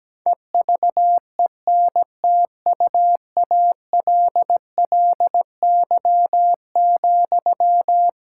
Morse code, heard as a keyed tone, about 17 wpm, 700 Hz. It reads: EVENTUALLY,
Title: Sentences from Top 1000 Words - Review Part 6 - 17wpm